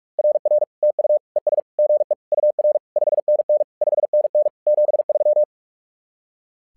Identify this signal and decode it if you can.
Morse code, heard as a keyed tone, about 46 wpm, 600 Hz. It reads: RR TU ES GE UR 5NN 5NN 73
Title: SOTA QSO Practice 45wpm